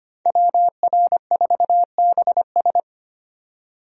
Morse code, heard as a keyed tone, about 25 wpm, 700 Hz. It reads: WR4BS
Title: USA Call Signs - 5 Characters - 25wpm